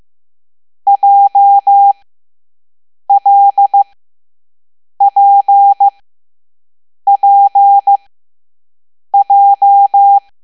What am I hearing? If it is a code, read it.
Morse code, heard as a keyed tone, about 15 wpm, spaced as for 6 wpm, 800 Hz. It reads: JLPPJ
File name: Morse / Level7B